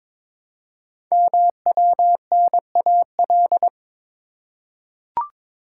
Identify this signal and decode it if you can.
Morse code, heard as a keyed tone, about 22 wpm, 700 Hz. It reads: MWNAL
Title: Code Groups - 5 Letters - 22wpm